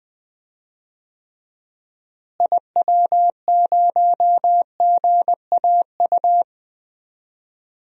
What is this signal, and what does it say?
Morse code, heard as a keyed tone, about 20 wpm, 700 Hz. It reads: IW0GAU
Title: DXCC-100 Call Signs - 6 Characters - Repeated 3x - 20wpm